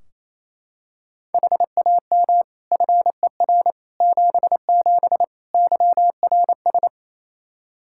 Morse code, heard as a keyed tone, about 28 wpm, 700 Hz. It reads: HAM FER 77 YRS
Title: K1USN SST QSO Practice 28wpm